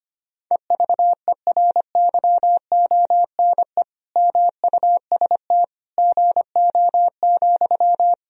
Morse code, heard as a keyed tone, about 25 wpm, 700 Hz. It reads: EVERYONE MUST GO,
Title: Sentences from Top 1000 Words - Review Part 4 - 25wpm